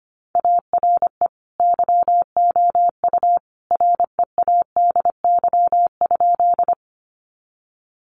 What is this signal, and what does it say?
Morse code, heard as a keyed tone, about 25 wpm, 700 Hz. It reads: ARE YOU READY?
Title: Sentences from Top 400 Words 25wpm